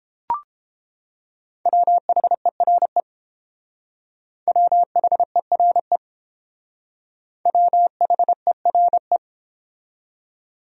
Morse code, heard as a keyed tone, about 31 wpm, 700 Hz. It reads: WHERE WHERE WHERE